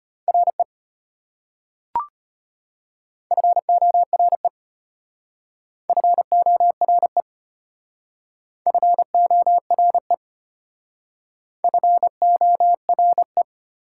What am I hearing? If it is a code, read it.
Morse code, heard as a keyed tone, about 34 wpm, 700 Hz. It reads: RE FORE FORE FORE FORE